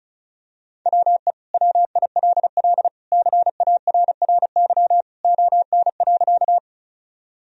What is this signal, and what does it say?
Morse code, heard as a keyed tone, about 34 wpm, 700 Hz. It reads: WE WILL CARRY ON.
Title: Sentences from Top 1000 Words - Review Part 1 - 35wpm